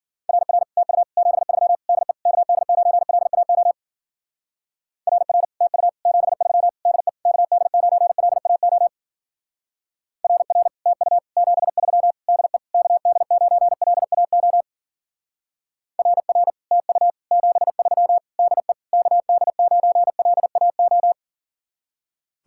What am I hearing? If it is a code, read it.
Morse code, heard as a keyed tone, about 51 wpm, 700 Hz. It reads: RR TU 73 DE KD9LAO RR TU 73 DE KD9LAO RR TU 73 DE KD9LAO RR TU 73 DE KD9LAO